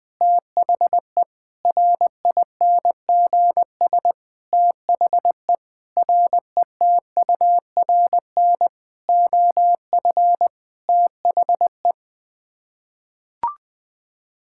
Morse code, heard as a keyed tone, about 20 wpm, 700 Hz. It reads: THE RINGS THE RETURN OF THE
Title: Sets of 9 Words - 20wpm